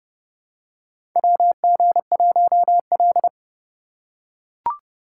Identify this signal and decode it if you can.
Morse code, heard as a keyed tone, about 30 wpm, 700 Hz. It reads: WG1L
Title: USA Call Signs - 2x Character Spacing - 30wpm